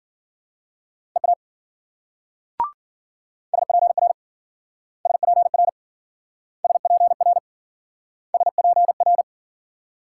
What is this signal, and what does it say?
Morse code, heard as a keyed tone, about 51 wpm, 700 Hz. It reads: EA SPR SPR SPR SPR